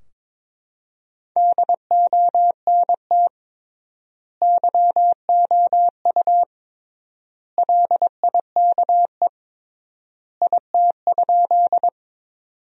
Morse code, heard as a keyed tone, about 22 wpm, 700 Hz. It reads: DONT YOU LIKE IT?
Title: 3x Word-Spacing - Sentences from Top 300 Words - 22wpm